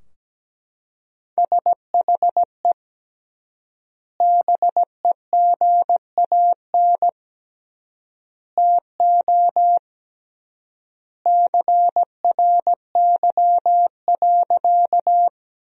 Morse code, heard as a keyed tone, about 17 wpm, 700 Hz. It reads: SHE BEGAN TO CRY.
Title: 3x Word-Spacing - Sentences from Top 500 Words - 17wpm